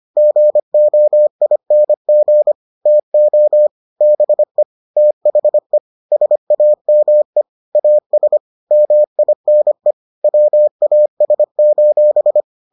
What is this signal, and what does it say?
Morse code, heard as a keyed tone, about 25 wpm, 600 Hz. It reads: GOING TO BE THE SAME AS MINE WAS: